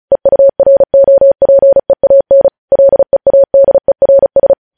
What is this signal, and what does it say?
Morse code, heard as a keyed tone, about 36 wpm, 550 Hz. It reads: EUROPEAN LEADERS
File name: podcast / News headlines-1x-35x35